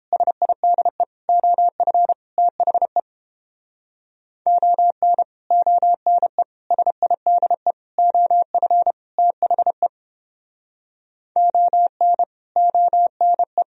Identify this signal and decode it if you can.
Morse code, heard as a keyed tone, about 29 wpm, 700 Hz. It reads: SIDE OF THE ON ONE SIDE OF THE ON ONE